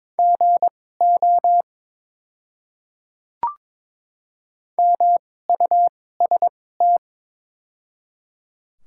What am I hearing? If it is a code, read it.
Morse code, heard as a keyed tone, about 22 wpm, spaced as for 16 wpm, 700 Hz. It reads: GO MUST
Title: 2x Character Spacing - Top 200 Words - 22wpm